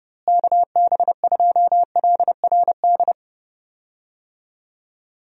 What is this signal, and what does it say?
Morse code, heard as a keyed tone, about 30 wpm, 700 Hz. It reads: KB2LRD
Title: USA Call Signs - 6 Characters - Repeated 3x - 30wpm